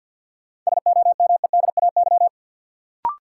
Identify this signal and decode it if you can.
Morse code, heard as a keyed tone, about 52 wpm, 700 Hz. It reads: SOMEDAY
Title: Sentences from Top 1600 Words - New Word Drill - 50wpm